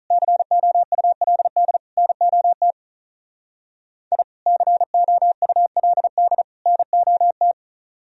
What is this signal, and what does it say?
Morse code, heard as a keyed tone, about 43 wpm, 700 Hz. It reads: COULD NOT I COULD NOT
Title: Speed-Racing - Sets of 3 Words 35wpm